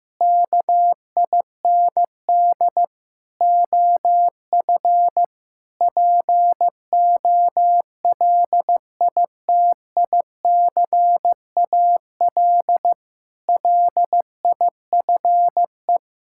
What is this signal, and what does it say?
Morse code, heard as a keyed tone, about 15 wpm, 700 Hz. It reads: KIND OF POLITICAL LIFE
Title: Sentences from Top 1000 Words - Review Part 5 - 15wpm